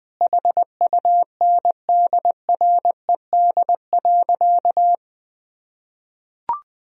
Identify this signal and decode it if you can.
Morse code, heard as a keyed tone, about 20 wpm, 700 Hz. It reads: HUNDRED.
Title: Sentences from Top 1000 Words - Review Part 2 - 20wpm